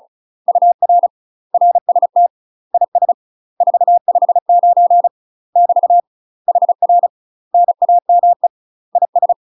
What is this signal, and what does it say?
Morse code, heard as a keyed tone, about 36 wpm, 700 Hz. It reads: UR RST IS 459 <BT> HR NAME IS
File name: UK MASTER QSOs TRAINING/35WPM